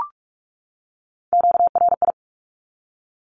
Morse code, heard as a keyed tone, about 44 wpm, 700 Hz. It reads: QRI
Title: Q-Codes 45wpm